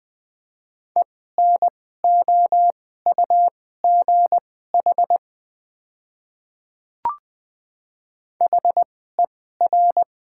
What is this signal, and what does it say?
Morse code, heard as a keyed tone, about 20 wpm, spaced as for 14 wpm, 700 Hz. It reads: ENOUGH HER
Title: 2x Character Spacing - Top 500 Words - 20wpm